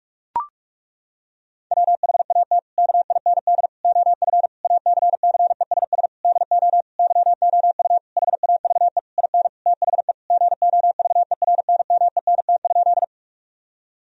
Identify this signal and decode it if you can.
Morse code, heard as a keyed tone, about 42 wpm, 700 Hz. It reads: WHAT KIND OF ACCESS DO YOU HAVE IN THE GOVERNMENT?
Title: Sentences from Top 1000 Words - Review Part 4 - 45wpm